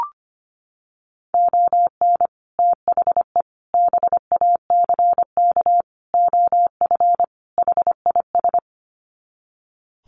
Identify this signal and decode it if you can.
Morse code, heard as a keyed tone, about 25 wpm, 700 Hz. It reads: ON THE BACK OF HIS
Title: Sets of 5 Words Encore 25wpm